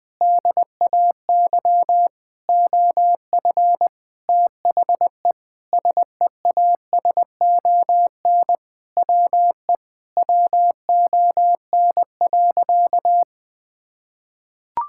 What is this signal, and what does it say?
Morse code, heard as a keyed tone, about 20 wpm, 700 Hz. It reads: DAY OF THE SEASON WE WON.
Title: Sentences with 7-Word Phrases - 20wpm